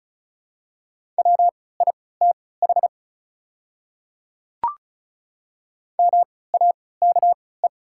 Morse code, heard as a keyed tone, about 34 wpm, spaced as for 20 wpm, 690 Hz. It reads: WITH MAKE
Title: 3x Character Spacing - Top 100 Words - 35wpm